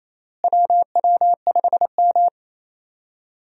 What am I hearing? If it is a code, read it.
Morse code, heard as a keyed tone, about 28 wpm, 700 Hz. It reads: WW5M
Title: USA Call Signs - 4 Characters - Repeated 2x - 28wpm